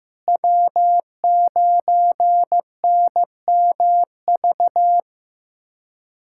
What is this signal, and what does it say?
Morse code, heard as a keyed tone, about 16 wpm, 700 Hz. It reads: W9NMV
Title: USA Callsign DE Callsign - 5 Characters - 15wpm